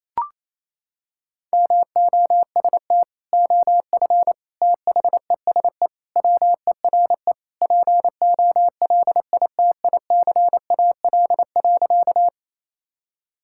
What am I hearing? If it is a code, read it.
Morse code, heard as a keyed tone, about 28 wpm, 700 Hz. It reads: MOST OF THESE WERE POLITICAL.
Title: Sentences from Top 1000 Words - Review Part 5 - 28wpm